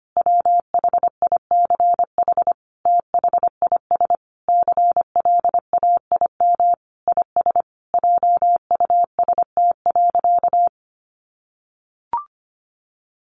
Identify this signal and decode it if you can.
Morse code, heard as a keyed tone, about 25 wpm, 700 Hz. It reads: WHICH THIS CLAIM IS JUST.
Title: Sentences from Top 1500 Words 25wpm